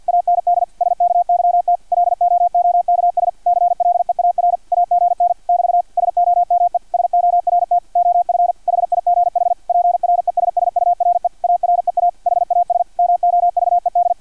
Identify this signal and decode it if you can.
Morse code, heard as a keyed tone, about 50 wpm, 700 Hz, with radio noise. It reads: AND SKYT LOOKS CLEAR AGN <BT> SOME SORT OF HIGH PRESSURE AREA HAS MOVED
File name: archived / CRSnet